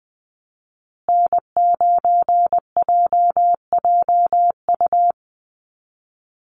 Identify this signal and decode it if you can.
Morse code, heard as a keyed tone, about 20 wpm, 700 Hz. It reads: N9JJU
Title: USA Call Signs Repeated 3x 20wpm